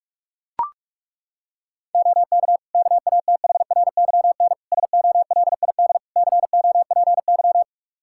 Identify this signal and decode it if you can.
Morse code, heard as a keyed tone, about 41 wpm, 700 Hz. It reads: OK KATHRYN SOLID COPY
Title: K1USN SST QSO Practice 45wpm